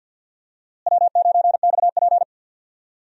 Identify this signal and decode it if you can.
Morse code, heard as a keyed tone, about 51 wpm, 700 Hz. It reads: W9XP
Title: USA Call Signs - 4 Characters - Repeated 3x - 50wpm